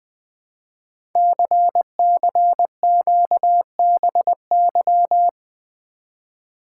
Morse code, heard as a keyed tone, about 20 wpm, 700 Hz. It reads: CCQBY
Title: Code Groups - 5 Letters - 20wpm